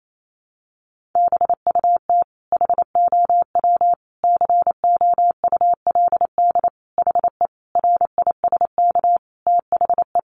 Morse code, heard as a keyed tone, about 28 wpm, 700 Hz. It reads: BUT HOW COULD HE RISK THE
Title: Sentences from Top 600 Words 28wpm